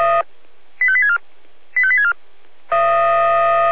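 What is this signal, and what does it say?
Five-tone selective call (ZVEI), heard as quick runs of five tones, tones of about 70 ms. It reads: 85864 85864
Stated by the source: ZVEI five-tone selective call received on an SDR.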